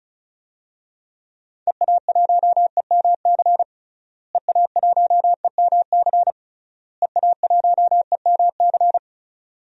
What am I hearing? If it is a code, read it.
Morse code, heard as a keyed tone, about 36 wpm, 700 Hz. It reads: EA1EMC EA1EMC EA1EMC
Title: DXCC-100 Call Signs - 6 Characters - Repeated 3x - 35wpm